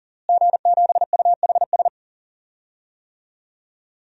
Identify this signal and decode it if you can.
Morse code, heard as a keyed tone, about 40 wpm, 700 Hz. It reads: G7UHS
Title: DXCC-100 Call Signs - 5 Characters - Repeated 3x - 40wpm